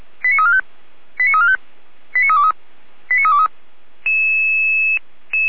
Five-tone selective call (ZVEI), heard as quick runs of five tones, tones of about 80 ms. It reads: 89345 89345 89323 89323